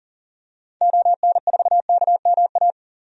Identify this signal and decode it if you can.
Morse code, heard as a keyed tone, about 40 wpm, 700 Hz. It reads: ON4KMA
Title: DXCC-100 Call Signs - 6 Characters - 40wpm